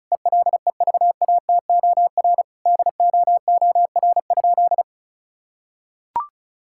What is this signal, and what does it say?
Morse code, heard as a keyed tone, about 36 wpm, 700 Hz. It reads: ELEVATOR DOOR?
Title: Sentences from Top 1600 Words 35wpm